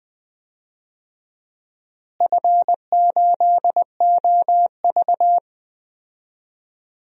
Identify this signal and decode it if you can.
Morse code, heard as a keyed tone, about 20 wpm, 700 Hz. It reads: F8OV
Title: DXCC-100 Call Signs - 4 Characters - Repeated 3x - 20wpm